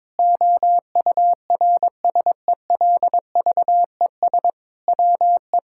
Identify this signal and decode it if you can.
Morse code, heard as a keyed tone, about 22 wpm, 700 Hz. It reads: OURSELVES WE
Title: Sentences from Top 1600 Words 22wpm